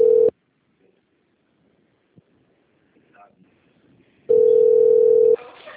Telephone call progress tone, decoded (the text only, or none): ringback tone